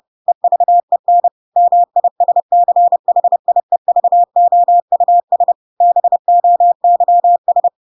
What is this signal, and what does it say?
Morse code, heard as a keyed tone, about 30 wpm, 700 Hz. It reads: EVEN MISCHIEVOUS BOYS
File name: Pinocchio / Pinocchio ch8 30 30 ik1hge